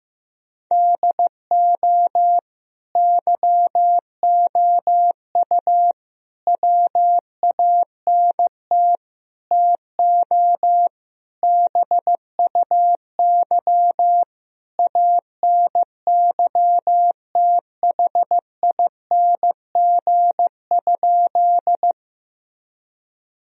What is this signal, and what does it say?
Morse code, heard as a keyed tone, about 15 wpm, 700 Hz. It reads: DO YOU WANT TO BUY ANYTHING?
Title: Sentences from Top 1000 Words - Review Part 4 - 15wpm